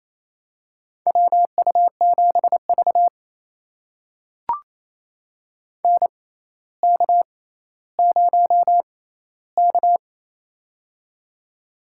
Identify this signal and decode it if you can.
Morse code, heard as a keyed tone, about 28 wpm, spaced as for 14 wpm, 700 Hz. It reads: WU7V NK0K
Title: USA Call Signs - 6x Character Spacing - 28wpm